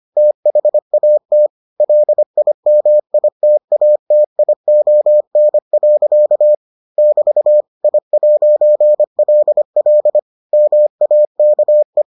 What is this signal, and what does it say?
Morse code, heard as a keyed tone, about 25 wpm, 600 Hz. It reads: THAT LIMITATION. <BT> I'LL MAKE